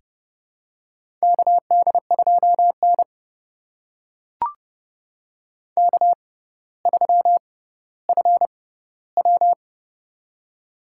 Morse code, heard as a keyed tone, about 29 wpm, spaced as for 15 wpm, 700 Hz. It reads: KD2N K3FW
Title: USA Call Signs - 4 Characters - 6x Character Spacing - 30wpm